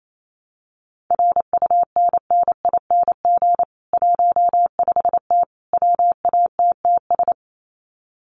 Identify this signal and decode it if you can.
Morse code, heard as a keyed tone, about 28 wpm, 700 Hz. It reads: RUNNING 15T WATTS
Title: Common QSO Phrases 28wpm